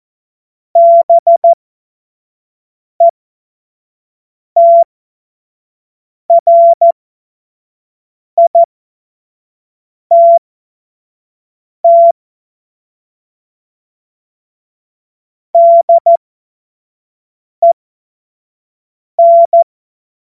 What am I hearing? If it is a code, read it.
Morse code, heard as a keyed tone, about 14 wpm, spaced as for 5 wpm, 680 Hz. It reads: BETRITT DEN
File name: Morsekurs / Witze / Witz 17. Mai 2022 25 F70